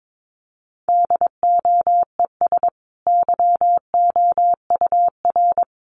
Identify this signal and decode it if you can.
Morse code, heard as a keyed tone, about 22 wpm, 700 Hz. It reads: DOES YOUR
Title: Sentences from Top 1200 Words 22wpm